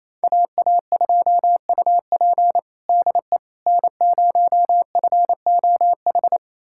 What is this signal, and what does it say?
Morse code, heard as a keyed tone, about 28 wpm, 700 Hz. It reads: AA2UP DE N0FOH
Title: USA Callsign DE Callsign - 5 Characters - 28wpm